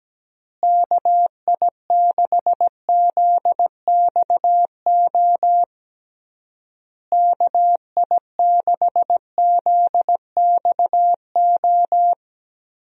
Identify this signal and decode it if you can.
Morse code, heard as a keyed tone, about 17 wpm, 700 Hz. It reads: KI6ZXO KI6ZXO
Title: USA Call Signs - 6 Characters - Repeated 3x - 17wpm